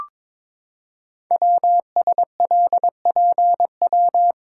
Morse code, heard as a keyed tone, about 22 wpm, 700 Hz. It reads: WSLPW
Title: Code Groups - 5 Letters - 22wpm